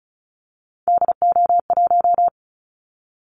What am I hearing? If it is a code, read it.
Morse code, heard as a keyed tone, about 36 wpm, 700 Hz. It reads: DO1
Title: Callsign Prefixes for Top 25 DXCC - 35wpm